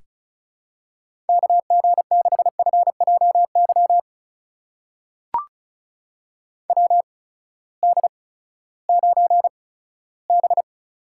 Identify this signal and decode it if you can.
Morse code, heard as a keyed tone, about 36 wpm, spaced as for 18 wpm, 700 Hz. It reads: KG6FJY WD9B
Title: USA Call Signs - 6 Characters - 8x Character Spacing - 35wpm